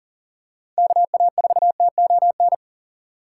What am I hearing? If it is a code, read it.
Morse code, heard as a keyed tone, about 40 wpm, 700 Hz. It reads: KA4TON
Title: USA Call Signs - 6 Characters - 6x Character Spacing - 40wpm